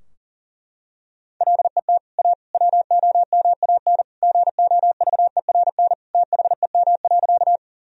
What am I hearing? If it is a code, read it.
Morse code, heard as a keyed tone, about 40 wpm, 700 Hz. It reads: LET A WOMAN GOVERN THEM.
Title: Sentences from Top 1000 Words - Review Part 2 - 40wpm